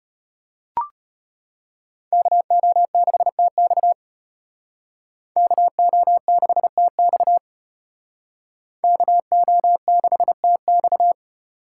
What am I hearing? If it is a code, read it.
Morse code, heard as a keyed tone, about 33 wpm, 700 Hz. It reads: KO6TX KO6TX KO6TX